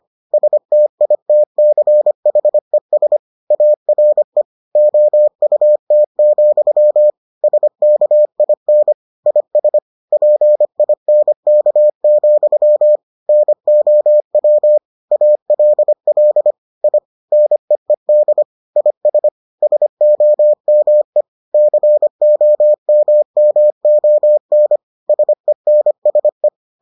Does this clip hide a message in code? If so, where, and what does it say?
Yes: Morse code from 0.3 to 26.5 s, heard as a keyed tone, about 25 wpm, 600 Hz. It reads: STITCHES ARE OUT, SKIN IS PINK, NOW ALL I NEED IS SOME COMMON SENSE